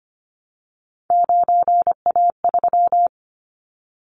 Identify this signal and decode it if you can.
Morse code, heard as a keyed tone, about 25 wpm, 700 Hz. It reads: 9A3
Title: Callsign Prefixes for Top 100 DXCC - 25wpm